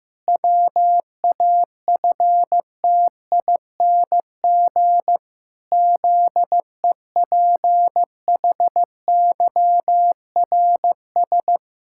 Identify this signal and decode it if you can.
Morse code, heard as a keyed tone, about 15 wpm, 700 Hz. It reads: WAFTING ZEPHYRS
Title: Pangram Challenge v2 - 15wpm